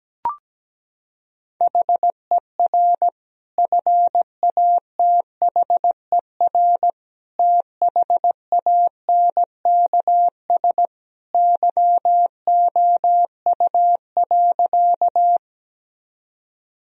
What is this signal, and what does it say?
Morse code, heard as a keyed tone, about 17 wpm, 700 Hz. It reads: HER FATHER THANKS YOU.
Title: Sentences from Top 1000 Words - Review Part 3 - 17wpm